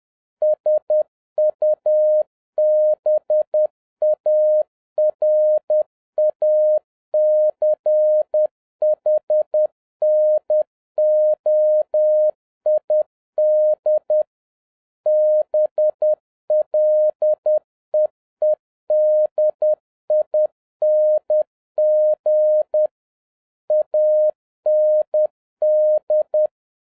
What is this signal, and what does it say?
Morse code, heard as a keyed tone, about 10 wpm, 610 Hz. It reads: SUBARACHNOID BLEEDING AND
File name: 2011-07-mp3 / n56 BACK FROM A BIG ACCIDENT 10wpm